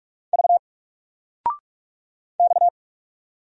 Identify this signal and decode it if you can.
Morse code, heard as a keyed tone, about 44 wpm, 700 Hz. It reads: V X
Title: Single Letter-Number Alt 45wpm